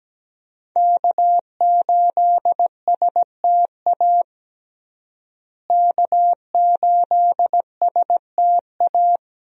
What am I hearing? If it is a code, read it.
Morse code, heard as a keyed tone, about 17 wpm, 700 Hz. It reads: K8STA K8STA
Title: USA Call Signs - 5 Characters - Repeated 2x - 17wpm